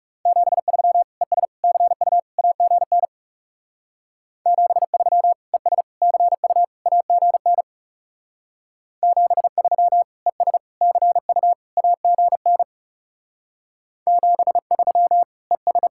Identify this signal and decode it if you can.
Morse code, heard as a keyed tone, about 41 wpm, 700 Hz. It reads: 73 ES CU AGN 73 ES CU AGN 73 ES CU AGN 73 ES